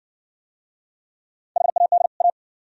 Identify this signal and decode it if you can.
Morse code, heard as a keyed tone, about 60 wpm, 690 Hz. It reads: HAD A